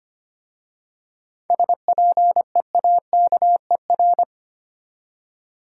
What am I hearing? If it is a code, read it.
Morse code, heard as a keyed tone, about 25 wpm, 700 Hz. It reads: SPEAKER
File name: practice / 25wpm / Easy Words with Hints - 25wpm